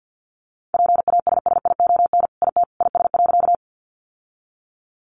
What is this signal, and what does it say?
Morse code, heard as a keyed tone, about 50 wpm, 700 Hz. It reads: PASSION IT IS.
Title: Sentences from Top 1300 Words 50wpm